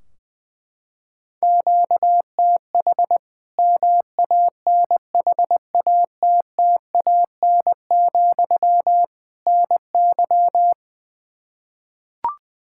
Morse code, heard as a keyed tone, about 20 wpm, 700 Hz. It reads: QTH MANHATTAN, NY
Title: Practice QSO - Exchange 1 - Beginning - 20wpm